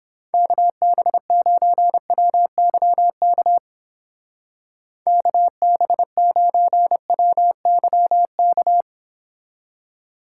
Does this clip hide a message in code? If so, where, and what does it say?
Yes: Morse code from 0.3 to 8.8 s, heard as a keyed tone, about 29 wpm, 700 Hz. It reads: KB9WYK KB9WYK